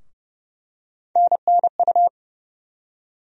Morse code, heard as a keyed tone, about 29 wpm, 700 Hz. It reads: NNU
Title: USA Call Sign Postfixes - 3 Letters 30wpm